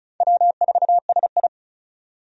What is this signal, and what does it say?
Morse code, heard as a keyed tone, about 34 wpm, 700 Hz. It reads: W4SI
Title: USA Call Signs - 4 Characters - 35wpm